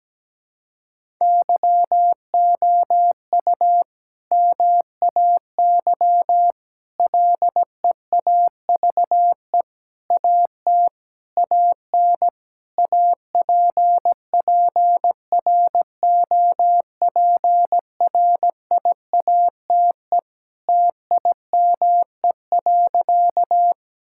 Morse code, heard as a keyed tone, about 17 wpm, 700 Hz. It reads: YOU MAY LEAVE AT AN APPROPRIATE TIME.
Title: Sentences from Top 1000 Words - Review Part 5 - 17wpm